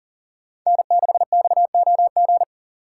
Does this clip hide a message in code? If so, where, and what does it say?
Yes: Morse code from 0.7 to 2.4 s, heard as a keyed tone, about 52 wpm, 700 Hz. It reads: N6XOG